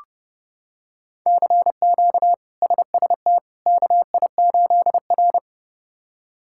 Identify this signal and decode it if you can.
Morse code, heard as a keyed tone, about 30 wpm, 700 Hz. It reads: CQ SST KI8R
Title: K1USN SST QSO Practice - 30wpm